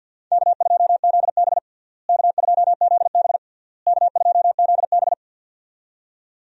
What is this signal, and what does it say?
Morse code, heard as a keyed tone, about 50 wpm, 700 Hz. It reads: K2ZB K2ZB K2ZB
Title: USA Call Signs - 4 Characters - Repeated 3x - 50wpm